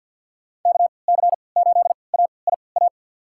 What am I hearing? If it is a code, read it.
Morse code, heard as a keyed tone, about 52 wpm, spaced as for 29 wpm, 700 Hz. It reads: KC8AIA
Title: USA Call Signs - 6 Characters - 3x Character Spacing - 50wpm